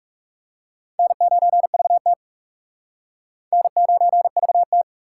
Morse code, heard as a keyed tone, about 36 wpm, 700 Hz. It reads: N9VT N9VT
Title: Speed Racing - USA 1x2 Call Signs 30wpm